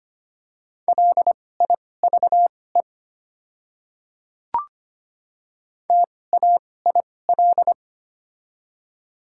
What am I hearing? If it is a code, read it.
Morse code, heard as a keyed tone, about 25 wpm, spaced as for 18 wpm, 700 Hz. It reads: LIVE TAIL
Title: 2x Character Spacing - Top 500 Words - 25wpm